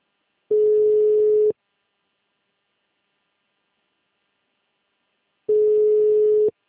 Ringback tone (Europe, 425 Hz).
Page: Conta Corrente - RJ